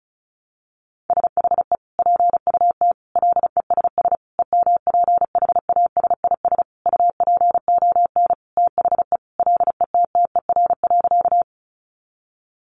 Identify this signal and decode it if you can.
Morse code, heard as a keyed tone, about 34 wpm, 700 Hz. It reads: SHE PUT LESS EMPHASIS UPON THE LETTER.
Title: Sentences from Top 1400 Words 35wpm